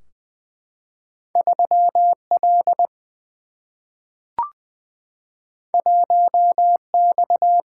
Morse code, heard as a keyed tone, about 20 wpm, 700 Hz. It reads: 3 L 1 X
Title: Single-Letter Single-Number 20wpm